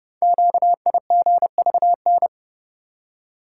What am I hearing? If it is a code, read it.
Morse code, heard as a keyed tone, about 30 wpm, 700 Hz. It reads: QIGVN
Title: Code Groups - 5 Letters - 30wpm